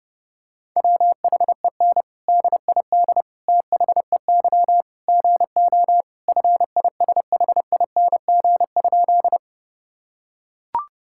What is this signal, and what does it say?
Morse code, heard as a keyed tone, about 30 wpm, 700 Hz. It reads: WHEN DID THEY GO FISHING?
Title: Sentences from Top 1000 Words - Review Part 5 - 30wpm